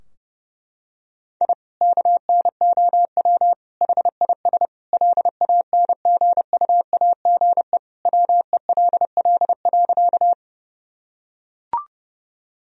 Morse code, heard as a keyed tone, about 30 wpm, 700 Hz. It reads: I KNOW HIS LANGUAGE WELL.
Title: Sentences from Top 1000 Words - Review Part 3 - 30wpm